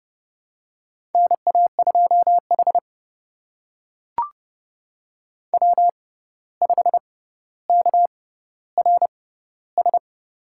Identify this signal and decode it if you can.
Morse code, heard as a keyed tone, about 30 wpm, spaced as for 14 wpm, 700 Hz. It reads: NA2H W5KRS
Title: USA Call Signs - 6x Character Spacing - 30wpm